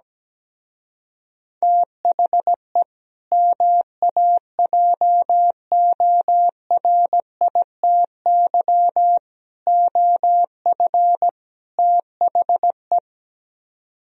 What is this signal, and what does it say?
Morse code, heard as a keyed tone, about 17 wpm, 700 Hz. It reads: THE MAJORITY OF THE
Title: Speed-Racing - Sets of 4 Words 17wpm